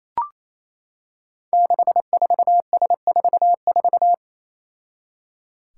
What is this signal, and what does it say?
Morse code, heard as a keyed tone, about 28 wpm, 700 Hz. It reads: 64S44
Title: Five Challenging Letter-Number Combinations 28wpm